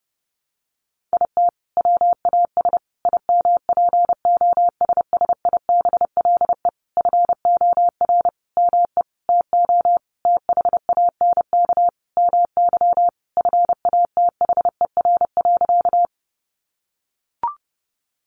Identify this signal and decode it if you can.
Morse code, heard as a keyed tone, about 30 wpm, 700 Hz. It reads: IT WAS IMPOSSIBLE FOR ME TO THANK MY FATHER.
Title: Sentences from Top 1100 Words 30wpm